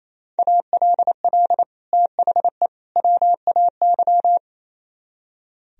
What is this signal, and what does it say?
Morse code, heard as a keyed tone, about 36 wpm, 700 Hz. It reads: ALL THE WAY